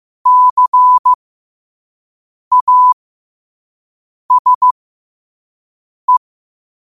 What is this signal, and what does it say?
Morse code, heard as a keyed tone, about 15 wpm, spaced as for 5 wpm, 1000 Hz. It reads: CASE
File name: qotdfiles / 5wpm